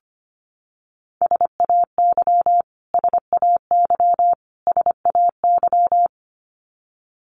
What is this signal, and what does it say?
Morse code, heard as a keyed tone, about 25 wpm, 700 Hz. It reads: SAY SAY SAY
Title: Top 200 Words Repeated 25wpm